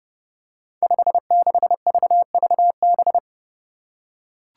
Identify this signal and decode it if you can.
Morse code, heard as a keyed tone, about 30 wpm, 700 Hz. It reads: 56VVB
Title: Five Challenging Letter-Number Combinations 30wpm